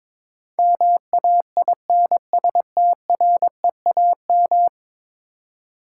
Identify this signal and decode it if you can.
Morse code, heard as a keyed tone, about 22 wpm, 700 Hz. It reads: MAINSTREAM
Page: morse-code-ninja-practice-22wpm